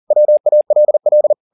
Morse code, heard as a keyed tone, about 40 wpm, 600 Hz. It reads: WALL
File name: the old man and the sea 2 40wpm